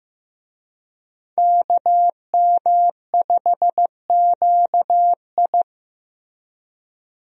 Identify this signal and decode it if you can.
Morse code, heard as a keyed tone, about 15 wpm, 700 Hz. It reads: KM5QI
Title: USA Call Signs - 5 Characters - Repeated 3x - 15wpm